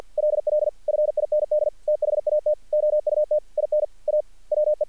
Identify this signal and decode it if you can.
Morse code, heard as a keyed tone, about 50 wpm, 600 Hz. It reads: LL FIND THAT OUT IN A WE
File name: archived / CRSnet